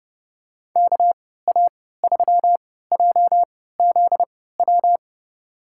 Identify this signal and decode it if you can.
Morse code, heard as a keyed tone, about 30 wpm, spaced as for 17 wpm, 700 Hz. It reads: KA3JZW
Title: USA Call Signs - 6 Characters - 3x Character Spacing - 30wpm